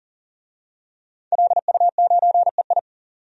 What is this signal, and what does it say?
Morse code, heard as a keyed tone, about 40 wpm, 700 Hz. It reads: LU9EI
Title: DXCC-100 Call Signs - 5 Characters - 40wpm